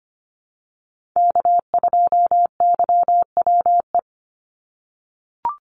Morse code, heard as a keyed tone, about 25 wpm, 700 Hz. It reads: K2YWE
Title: Call Signs - 25wpm